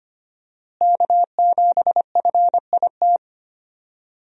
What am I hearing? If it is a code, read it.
Morse code, heard as a keyed tone, about 25 wpm, 700 Hz. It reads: K7FIT
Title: USA Call Signs - 5 Characters - Repeated 3x - 25wpm